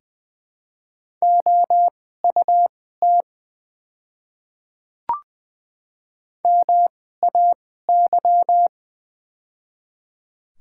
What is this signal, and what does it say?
Morse code, heard as a keyed tone, about 20 wpm, spaced as for 14 wpm, 700 Hz. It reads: OUT MAY
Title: 2x Character Spacing - Top 100 Words - 20wpm